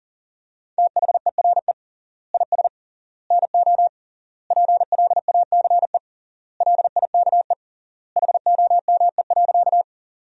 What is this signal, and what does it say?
Morse code, heard as a keyed tone, about 40 wpm, 700 Hz. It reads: THERE IS NO PLACE LIKE HOME.
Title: 3x Word-Spacing - Sentences from Top 200 Words - 40wpm